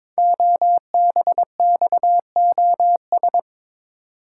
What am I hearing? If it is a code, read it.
Morse code, heard as a keyed tone, about 22 wpm, 700 Hz. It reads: OBXOS
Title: Code Groups - 5 Letters - 22wpm